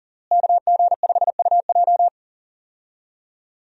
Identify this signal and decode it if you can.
Morse code, heard as a keyed tone, about 40 wpm, 700 Hz. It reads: KG5UJ
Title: USA Call Signs - 5 Characters - Repeated 3x - 40wpm